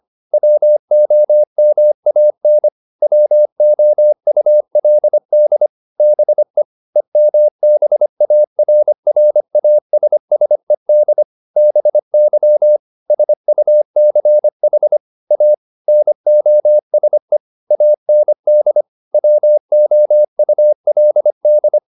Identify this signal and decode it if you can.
Morse code, heard as a keyed tone, about 25 wpm, 600 Hz. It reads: WOMAN WOULD BE EMBARRASSED BY SUCH A NOSE AND WOULD